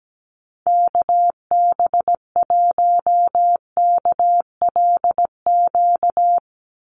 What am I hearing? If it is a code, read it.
Morse code, heard as a keyed tone, about 17 wpm, 700 Hz. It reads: KB1KLQ
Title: USA Call Signs 17wpm